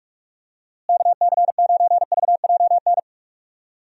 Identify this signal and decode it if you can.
Morse code, heard as a keyed tone, about 46 wpm, 700 Hz. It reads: KC9VJN
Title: USA Call Signs - 6 Characters - Repeated 2x - 45wpm